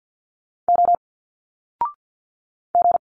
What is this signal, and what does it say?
Morse code, heard as a keyed tone, about 50 wpm, 700 Hz. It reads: O G
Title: Single Letters 50wpm